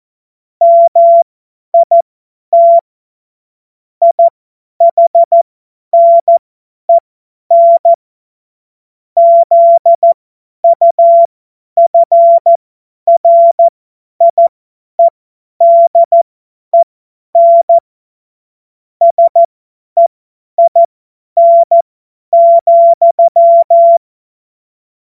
Morse code, heard as a keyed tone, about 14 wpm, spaced as for 10 wpm, 680 Hz. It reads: MIT IHNEN ZUFRIEDEN SEIN,
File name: Morsekurs / Witze / Witz 13. Dezember 2022 50 F70